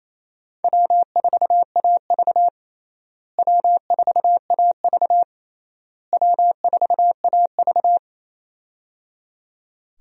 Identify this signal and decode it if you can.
Morse code, heard as a keyed tone, about 28 wpm, 700 Hz. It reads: W4AV W4AV W4AV
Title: USA Call Signs - 4 Characters - Repeated 3x - 28wpm